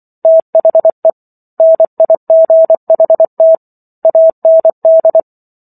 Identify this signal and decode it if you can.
Morse code, heard as a keyed tone, about 24 wpm, 650 Hz. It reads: THE NIGHT AND